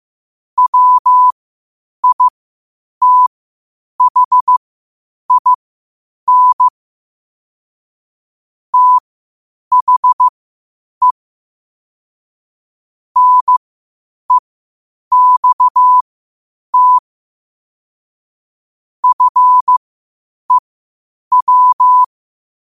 Morse code, heard as a keyed tone, about 15 wpm, spaced as for 8 wpm, 1000 Hz. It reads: WITHIN THE NEXT FEW
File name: qotdfiles / 8wpm